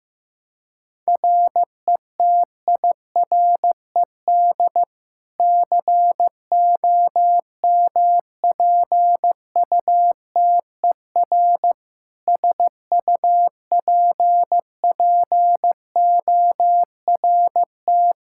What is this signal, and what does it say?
Morse code, heard as a keyed tone, about 15 wpm, 700 Hz. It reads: RETIRED COMPUTER SUPPORT
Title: Practice QSO - Exchange 3 - More Info - 15wpm